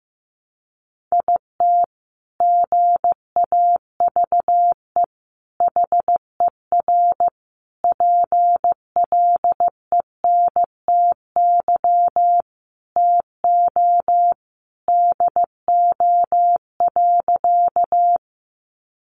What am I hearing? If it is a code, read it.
Morse code, heard as a keyed tone, about 14 wpm, 700 Hz. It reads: IT GAVE HER PLENTY TO DO.
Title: Sentences from Top 1300 Words 15wpm